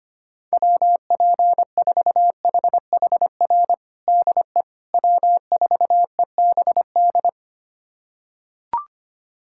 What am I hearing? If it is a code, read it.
Morse code, heard as a keyed tone, about 25 wpm, 700 Hz. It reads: WP4HHR DE W4EBD
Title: USA Callsign DE Callsign 25wpm